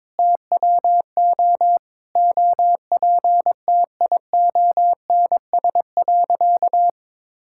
Morse code, heard as a keyed tone, about 21 wpm, 700 Hz. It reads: TWO OPTIONS.
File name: practice / 22wpm / Sentences with 3-Word Phrases - 22wpm